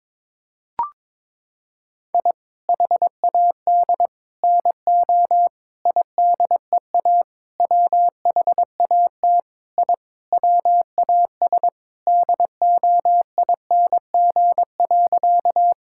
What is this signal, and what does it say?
Morse code, heard as a keyed tone, about 22 wpm, 700 Hz. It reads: I HAD NO IDEA WHAT I WAS DOING.
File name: practice / 22wpm / Sentences with 8-Word Phrases - 22wpm